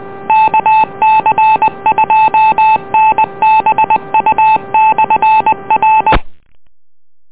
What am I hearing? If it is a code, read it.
Morse code, heard as a keyed tone, about 20 wpm, 870 Hz. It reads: KC2NBU/R